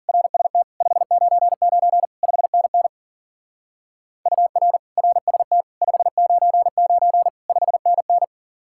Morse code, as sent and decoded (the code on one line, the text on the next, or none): .-. ... - / ..... ----. ----. / ..... -. -. / ..- .-. / .-. ... - / ..... ----. ----. / ..... -. -.
RST 599 5NN UR RST 599 5NN